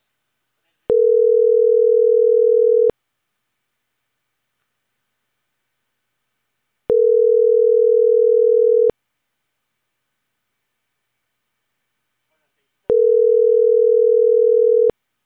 Ringback tone (North America), 440 + 480 Hz.